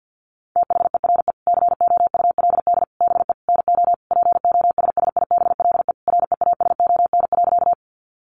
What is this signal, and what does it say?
Morse code, heard as a keyed tone, about 50 wpm, 700 Hz. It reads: THERE COULD BE NO POSSIBLE REASON.
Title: Sentences from Top 700 Words 50wpm